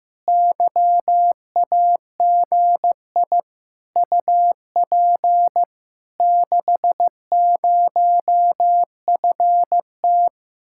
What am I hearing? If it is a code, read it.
Morse code, heard as a keyed tone, about 15 wpm, 700 Hz. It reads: YAGI UP 60FT
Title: Speed-Racing Common QSO Phrases Encore - 15wpm